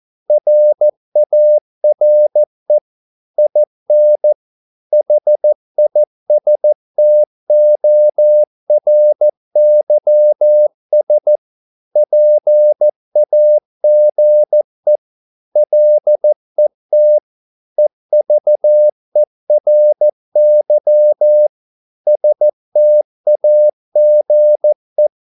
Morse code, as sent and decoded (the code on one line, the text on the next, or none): .-. .- .-. / .. -. / .... .. ... - --- .-. -.-- ... / .--. .- --. / .-.. - / . ...- . .-. -.-- / ... - .- --. .
RARE IN HISTORYS PAGE LET EVERY STAGE